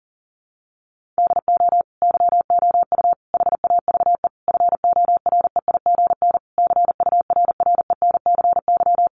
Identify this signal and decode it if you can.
Morse code, heard as a keyed tone, about 40 wpm, 700 Hz. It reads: DO YOU HAVE FOREIGN CURRENCY
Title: Sentences from Top 1500 Words 40wpm